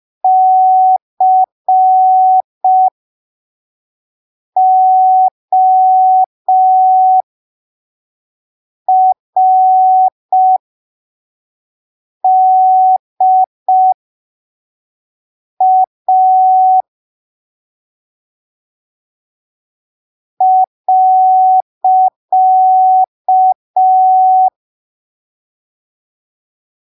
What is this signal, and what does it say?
Morse code, heard as a keyed tone, about 5 wpm, spaced as for 3 wpm, 750 Hz. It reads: CORDA.